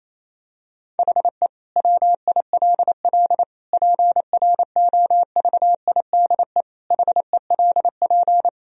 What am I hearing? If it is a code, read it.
Morse code, heard as a keyed tone, about 28 wpm, 700 Hz. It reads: HE WILL PROVIDE HELP
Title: Sentences from Top 1000 Words - Review Part 3 - 28wpm